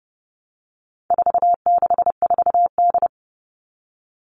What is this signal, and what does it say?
Morse code, heard as a keyed tone, about 30 wpm, 700 Hz. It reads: 464D
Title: All Cut Numbers - 4-Digit 30wpm